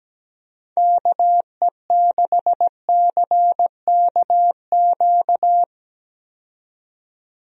Morse code, heard as a keyed tone, about 17 wpm, 700 Hz. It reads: KE6CKQ
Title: USA Call Signs - 6 Characters - 17wpm